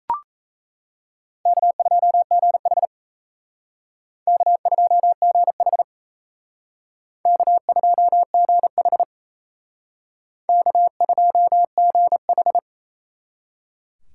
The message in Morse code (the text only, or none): K2GH K2GH K2GH K2GH